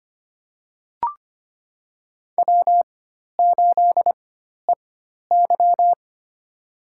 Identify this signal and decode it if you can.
Morse code, heard as a keyed tone, about 25 wpm, spaced as for 12 wpm, 700 Hz. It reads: W8EY
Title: USA Call Signs - 4 Characters - 4x Character Spacing - 25wpm